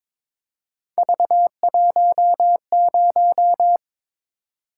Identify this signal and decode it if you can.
Morse code, heard as a keyed tone, about 22 wpm, 700 Hz. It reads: V10
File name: practice / 22wpm / Three Letter-Number Combinations - 22wpm